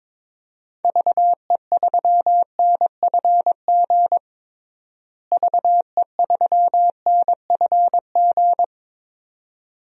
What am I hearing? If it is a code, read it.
Morse code, heard as a keyed tone, about 21 wpm, 700 Hz. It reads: VE3NFG VE3NFG
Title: DXCC-100 Call Signs - 6 Characters - Repeated 2x - 22wpm